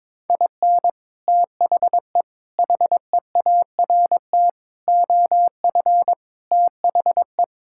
Morse code, heard as a keyed tone, about 22 wpm, 700 Hz. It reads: IN THE HEART OF THE
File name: practice / 22wpm / Sentences with 3-Word Phrases - 22wpm